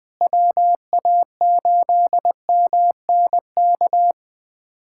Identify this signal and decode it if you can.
Morse code, heard as a keyed tone, about 20 wpm, 700 Hz. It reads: WA8MNK
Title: USA Callsign DE Callsign 20wpm